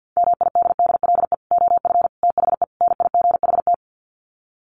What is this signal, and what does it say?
Morse code, heard as a keyed tone, about 50 wpm, 700 Hz. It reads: MIDDLE OF THE NIGHT
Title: Sets of 4 Words 50wpm